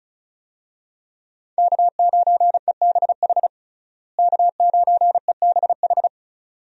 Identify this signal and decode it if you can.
Morse code, heard as a keyed tone, about 36 wpm, 700 Hz. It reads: K9EBH K9EBH
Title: USA Call Signs - 5 Characters - Repeated 2x - 35wpm